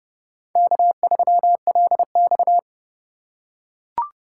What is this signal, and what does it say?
Morse code, heard as a keyed tone, about 30 wpm, 700 Hz. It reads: K3LX
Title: USA Call Signs - 4 Characters - 2x Character Spacing - 30wpm